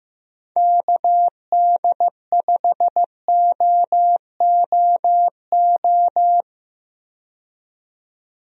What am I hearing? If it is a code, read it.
Morse code, heard as a keyed tone, about 15 wpm, 700 Hz. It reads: KD5OOO
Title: USA Call Signs - 6 Characters - 15wpm